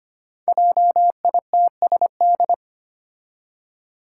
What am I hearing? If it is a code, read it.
Morse code, heard as a keyed tone, about 25 wpm, 700 Hz. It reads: JITSD
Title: Five Challenging Letter-Number Combinations 25wpm